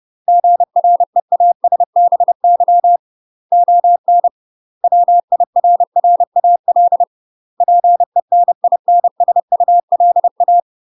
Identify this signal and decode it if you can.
Morse code, heard as a keyed tone, about 30 wpm, 700 Hz. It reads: GREASBY ON WIRRAL PENINSULA
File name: UK MASTER QSOs TRAINING / 30WPM / UK MASTER QSOs 0000